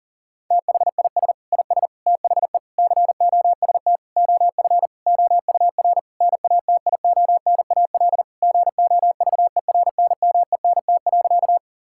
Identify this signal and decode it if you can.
Morse code, heard as a keyed tone, about 40 wpm, 700 Hz. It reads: THIS IS THE COST OF OUR NATIONAL GOVERNMENT.
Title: Sentences from Top 1000 Words - Review Part 5 - 40wpm